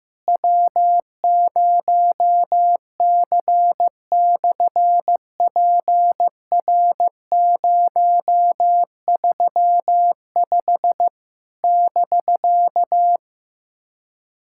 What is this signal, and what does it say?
Morse code, heard as a keyed tone, about 15 wpm, 700 Hz. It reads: W0C/PR035 <BK>
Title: SOTA QSO Practice 15wpm